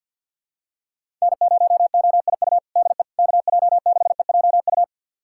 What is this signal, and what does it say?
Morse code, heard as a keyed tone, about 47 wpm, 700 Hz. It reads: N0OIU DE KJ6EJU
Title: USA Callsign DE Callsign 50wpm